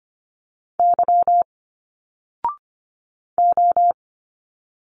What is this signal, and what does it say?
Morse code, heard as a keyed tone, about 25 wpm, 700 Hz. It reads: Y O
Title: Single Letters 25wpm